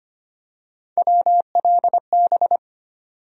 Morse code, heard as a keyed tone, about 25 wpm, 700 Hz. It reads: WLB
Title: USA Call Sign Postfixes - 3 Letters 25wpm